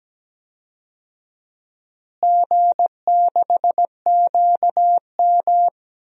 Morse code, heard as a keyed tone, about 17 wpm, 700 Hz. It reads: G6QM
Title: DXCC-100 Call Signs - 4 Characters - Repeated 2x - 17wpm